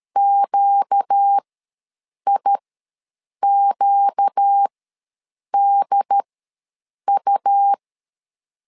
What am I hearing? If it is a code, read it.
Morse code, heard as a keyed tone, about 13 wpm, spaced as for 7 wpm, 800 Hz. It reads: QIQDU
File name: LEZIONI / 4 Q ws 12wpm 1912